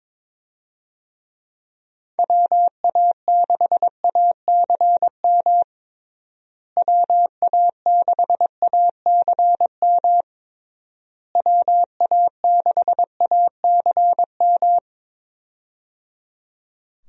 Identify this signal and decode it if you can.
Morse code, heard as a keyed tone, about 21 wpm, 700 Hz. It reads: WA6ACM WA6ACM WA6ACM
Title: USA Call Signs - 6 Characters - Repeated 3x - 22wpm